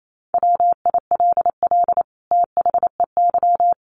Morse code, heard as a keyed tone, about 28 wpm, 700 Hz. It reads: WILL THEY
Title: Sentences from Top 1400 Words 28wpm